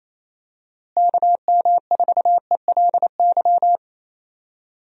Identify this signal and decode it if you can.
Morse code, heard as a keyed tone, about 28 wpm, 700 Hz. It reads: KM4ELY
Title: USA Call Signs - 6 Characters - Repeated 2x - 28wpm